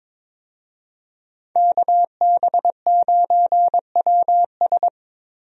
Morse code, heard as a keyed tone, about 22 wpm, 700 Hz. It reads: KB9WS
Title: USA Call Signs - 5 Characters - Repeated 2x - 22wpm